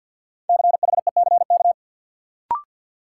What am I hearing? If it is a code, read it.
Morse code, heard as a keyed tone, about 53 wpm, 700 Hz. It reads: CHECK
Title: Sentences from Top 1000 Words - Review Part 3 - New Word Drill - 50wpm